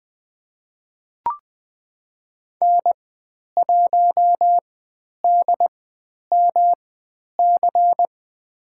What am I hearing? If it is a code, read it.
Morse code, heard as a keyed tone, about 21 wpm, spaced as for 10 wpm, 700 Hz. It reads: N1DMC
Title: 109 Lesson 18g - Letter m - Call Signs 20wpm